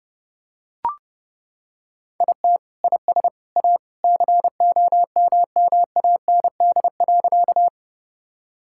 Morse code, heard as a keyed tone, about 30 wpm, 700 Hz. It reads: IT IS A COMMAND.
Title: Sentences from Top 1600 Words 30wpm